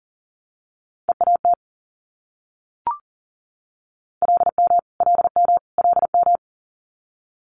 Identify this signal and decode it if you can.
Morse code, heard as a keyed tone, about 38 wpm, 690 Hz. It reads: EAT LM LM LM
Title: Multi-letter Phonemes Repeated 3x 40wpm